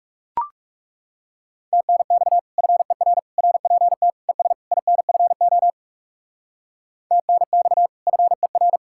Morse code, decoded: TNX FER RPT ES INFO TNX FER

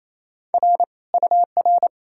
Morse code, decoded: R UR

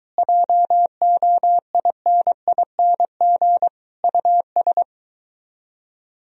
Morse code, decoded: JOINING US

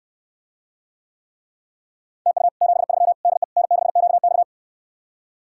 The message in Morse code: TU 73 DE N6QX

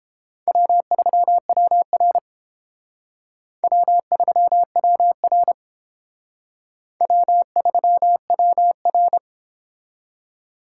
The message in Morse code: W3WR W3WR W3WR

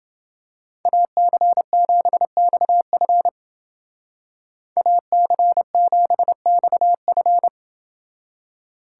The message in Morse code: AC7XF AC7XF

